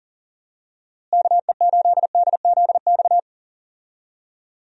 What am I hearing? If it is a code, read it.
Morse code, heard as a keyed tone, about 40 wpm, 700 Hz. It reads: KE8DZX
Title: USA Call Signs - 6 Characters - Repeated 3x - 40wpm